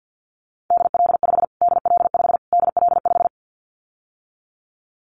Morse code, heard as a keyed tone, about 50 wpm, 700 Hz. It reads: DL5 DL5 DL5
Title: Callsign Prefixes for Top 25 DXCC - Repeated 3x - 50wpm